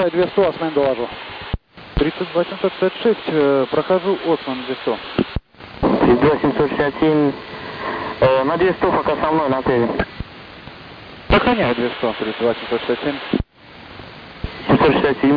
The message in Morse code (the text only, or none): none